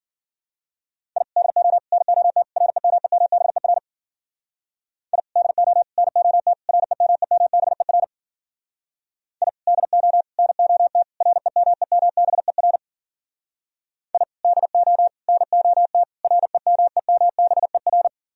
Morse code, heard as a keyed tone, about 50 wpm, 700 Hz. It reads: I DO NOT REMEMBER I DO NOT REMEMBER I DO NOT REMEMBER I DO NOT REMEMBER